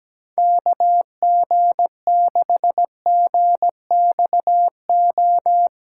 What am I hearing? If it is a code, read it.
Morse code, heard as a keyed tone, about 17 wpm, 700 Hz. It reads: KG6GXO
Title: USA Callsign DE Callsign - 6 Characters - 17wpm